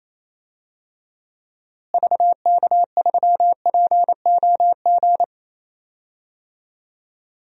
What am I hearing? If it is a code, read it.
Morse code, heard as a keyed tone, about 28 wpm, 700 Hz. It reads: VK3POG
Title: DXCC-100 Call Signs - 6 Characters - Repeated 2x - 28wpm